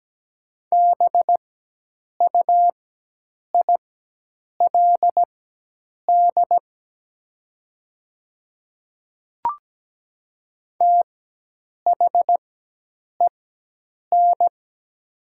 Morse code, heard as a keyed tone, about 17 wpm, spaced as for 8 wpm, 700 Hz. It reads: BUILD THEN
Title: 4x Character Spacing - Top 500 Words - 17wpm